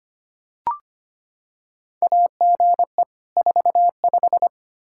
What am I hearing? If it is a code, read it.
Morse code, heard as a keyed tone, about 25 wpm, 700 Hz. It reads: AGE 45
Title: Practice QSO - Exchange 3 - More Info - 25wpm